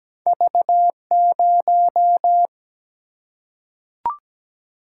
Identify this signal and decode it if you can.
Morse code, heard as a keyed tone, about 17 wpm, 700 Hz. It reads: V0
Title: Single-Letter Single-Number 17wpm